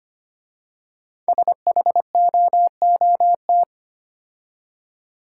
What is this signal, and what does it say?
Morse code, heard as a keyed tone, about 25 wpm, 700 Hz. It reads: SHOOT
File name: 5 Letter Words - First 3 Letters Given - 25wpm